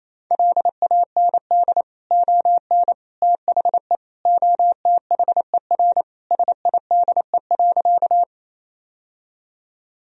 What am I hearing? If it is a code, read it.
Morse code, heard as a keyed tone, about 27 wpm, 700 Hz. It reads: LAND ON THE OTHER SIDE.
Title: Sentences from Top 1000 Words - Review Part 1 - 28wpm